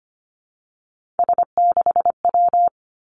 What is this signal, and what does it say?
Morse code, heard as a keyed tone, about 25 wpm, 700 Hz. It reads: S6W
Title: Three Challenging Letter-Number Combinations 25wpm